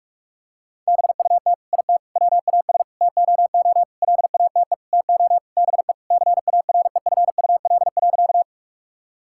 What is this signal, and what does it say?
Morse code, heard as a keyed tone, about 46 wpm, 700 Hz. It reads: BUT IT WAS TOO LATE TO BE CAREFUL.
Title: Sentences from Top 1600 Words 45wpm